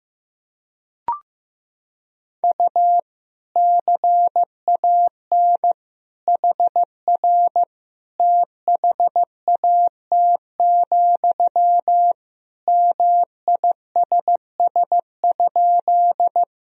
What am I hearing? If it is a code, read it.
Morse code, heard as a keyed tone, about 15 wpm, 700 Hz. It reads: U CAN HR THAT, MISS?
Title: Sentences from Top 500 Words and Abbreviations 15wpm